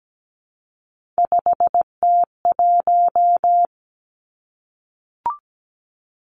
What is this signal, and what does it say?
Morse code, heard as a keyed tone, about 17 wpm, 700 Hz. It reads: 5T1
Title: Three Challenging Letter-Number Combinations 17wpm